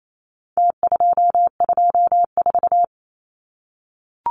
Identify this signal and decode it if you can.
Morse code, heard as a keyed tone, about 28 wpm, 700 Hz. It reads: T224
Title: Numbers - Contest Style - 4-Digit 28wpm